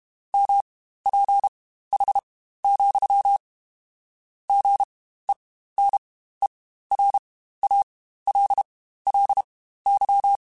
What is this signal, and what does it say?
Morse code, heard as a keyed tone, about 32 wpm, spaced as for 15 wpm, 800 Hz. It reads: MPH, GENERALLY
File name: Spaced Characters / Spaced Characters-mp3s / Weather 5 12 sp @30